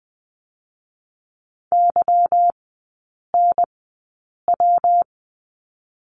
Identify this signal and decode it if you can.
Morse code, heard as a keyed tone, about 21 wpm, 690 Hz. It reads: Y N W
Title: 145 Lesson 24e - Letter y - Three Characters 20wpm